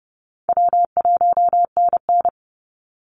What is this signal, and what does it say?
Morse code, heard as a keyed tone, about 28 wpm, 700 Hz. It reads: W1NN
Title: Easy Call Signs Repeated 3x - 30wpm